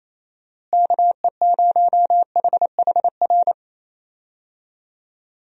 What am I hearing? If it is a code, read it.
Morse code, heard as a keyed tone, about 27 wpm, 700 Hz. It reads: KE0HHR
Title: USA Call Signs - 6 Characters - Repeated 3x - 28wpm